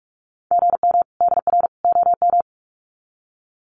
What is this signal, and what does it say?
Morse code, heard as a keyed tone, about 44 wpm, 700 Hz. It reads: GM DR OM
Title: Common QSO Phrases 45wpm